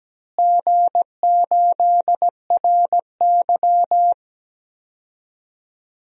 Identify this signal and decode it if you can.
Morse code, heard as a keyed tone, about 17 wpm, 700 Hz. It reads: G8RY
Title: DXCC-100 Call Signs - 4 Characters - Repeated 2x - 17wpm